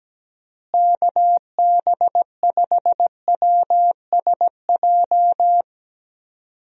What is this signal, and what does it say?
Morse code, heard as a keyed tone, about 17 wpm, 700 Hz. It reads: KB5WSJ
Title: USA Call Signs - 6 Characters - Repeated 3x - 17wpm